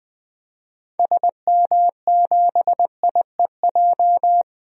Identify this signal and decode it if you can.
Morse code, heard as a keyed tone, about 21 wpm, 700 Hz. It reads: SM7IEJ
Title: DXCC-100 Call Signs - 6 Characters - 20wpm